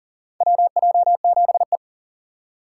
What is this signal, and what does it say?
Morse code, heard as a keyed tone, about 40 wpm, 700 Hz. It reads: WJ7E
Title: USA Call Signs - 4 Characters - 2x Character Spacing - 40wpm